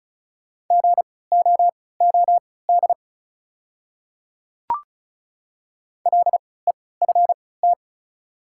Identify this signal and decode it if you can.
Morse code, heard as a keyed tone, about 36 wpm, spaced as for 20 wpm, 700 Hz. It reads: GOOD LEFT
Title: 3x Character Spacing - Top 300 Words - 35wpm